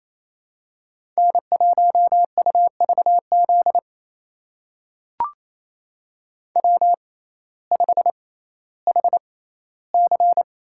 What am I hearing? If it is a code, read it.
Morse code, heard as a keyed tone, about 28 wpm, spaced as for 15 wpm, 700 Hz. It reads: N1UVZ W5HC
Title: USA Call Signs - 5 Characters - 6x Character Spacing - 28wpm